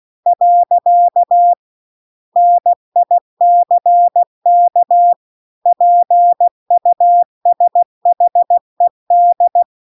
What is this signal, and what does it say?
Morse code, heard as a keyed tone, about 16 wpm, 700 Hz. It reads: . NICK PUSHED
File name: CW-The-Big-Twoheaded-River-2-16WPM